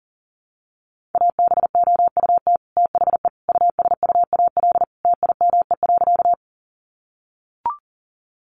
Morse code, heard as a keyed tone, about 40 wpm, 700 Hz. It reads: ABOUT THE USUAL TIME.
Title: Sentences from Top 400 Words 40wpm